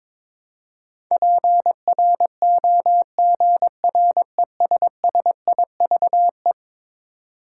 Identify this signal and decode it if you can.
Morse code, heard as a keyed tone, about 22 wpm, 700 Hz. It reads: PROGRESSIVE
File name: practice / 22wpm / Eleven Letter Words - 22wpm